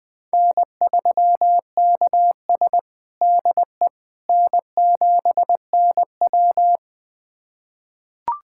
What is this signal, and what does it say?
Morse code, heard as a keyed tone, about 20 wpm, 700 Hz. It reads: N3KS DE N7NW
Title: USA Callsign DE Callsign - 4 Characters - 20wpm